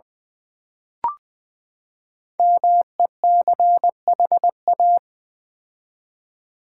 Morse code, heard as a keyed tone, about 20 wpm, 700 Hz. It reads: MECHA
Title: Build A Word - 5 Letters At A Time - 10-Letter Words - 20wpm